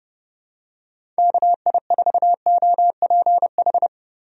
Morse code, heard as a keyed tone, about 32 wpm, 700 Hz. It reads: KI4OPH
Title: USA Call Signs - 6 Characters - Repeated 2x - 30wpm